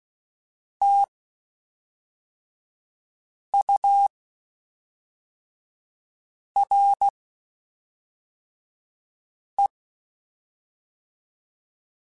Morse code, heard as a keyed tone, about 16 wpm, 800 Hz. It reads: T U R E
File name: Spaced Characters / Spaced Characters-mp3s / Weather5 22 sp.@5